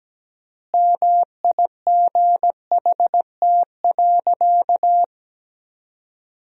Morse code, heard as a keyed tone, about 17 wpm, 700 Hz. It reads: MIGHT.
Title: 3x Word-Spacing - Sentences from Top 500 Words - 17wpm